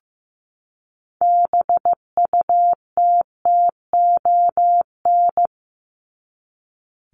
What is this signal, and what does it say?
Morse code, heard as a keyed tone, about 15 wpm, 700 Hz. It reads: BUTTON
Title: Top 1001-1250 Words 15wpm